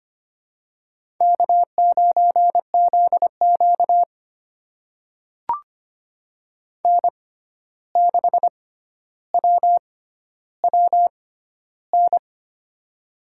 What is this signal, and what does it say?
Morse code, heard as a keyed tone, about 25 wpm, spaced as for 12 wpm, 700 Hz. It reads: K9ZQ N6WWN
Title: USA Call Signs - 6x Character Spacing - 25wpm